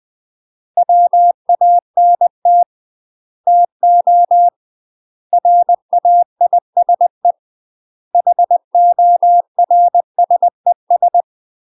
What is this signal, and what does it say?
Morse code, heard as a keyed tone, about 20 wpm, 700 Hz. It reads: WANT TO RAISE HORSES
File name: UK MASTER QSOs TRAINING / 20WPM / UK MASTER QSOs 0032